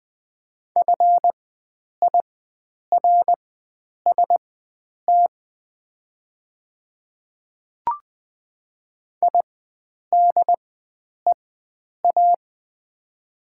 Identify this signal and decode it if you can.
Morse code, heard as a keyed tone, about 20 wpm, spaced as for 9 wpm, 700 Hz. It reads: FIRST IDEA
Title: 4x Character Spacing - Top 500 Words - 20wpm